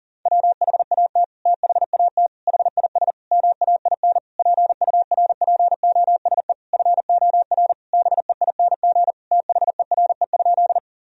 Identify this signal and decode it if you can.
Morse code, heard as a keyed tone, about 40 wpm, 700 Hz. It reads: WHAT THAT HIS MAIN PURPOSE FOR BEING THERE?
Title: Sentences from Top 1000 Words - Review Part 2 - 40wpm